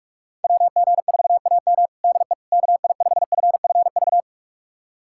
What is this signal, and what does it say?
Morse code, heard as a keyed tone, about 41 wpm, 700 Hz. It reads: WG4AM DE KI5FFV
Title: USA Callsign DE Callsign 45wpm